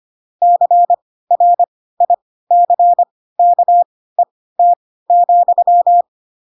Morse code, heard as a keyed tone, about 25 wpm, spaced as for 16 wpm, 700 Hz. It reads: CRICKET,
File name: Pinocchio / Pinocchio ch4 10 25 ik1hge